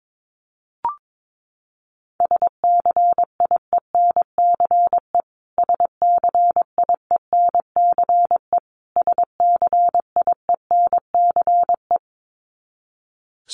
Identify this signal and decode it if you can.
Morse code, heard as a keyed tone, about 22 wpm, 700 Hz. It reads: SCIENCE SCIENCE SCIENCE
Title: Top 500 Words Repeated 22wpm